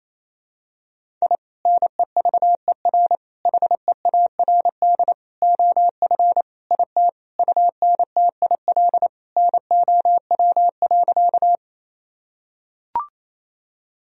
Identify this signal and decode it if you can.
Morse code, heard as a keyed tone, about 28 wpm, 700 Hz. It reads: I NEVER HEARD OF IT UNTIL NOW.
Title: Sentences from Top 1000 Words - Review Part 1 - 28wpm